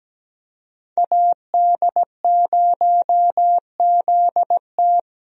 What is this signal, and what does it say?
Morse code, heard as a keyed tone, about 17 wpm, 700 Hz. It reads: AD0ZT
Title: USA Call Signs - 5 Characters - 17wpm